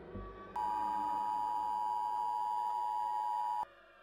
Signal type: Emergency Alert System (SAME) tones